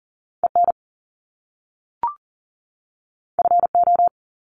Morse code, heard as a keyed tone, about 40 wpm, 700 Hz. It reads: EN FO
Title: Top Two Letter Combinations in Words 40wpm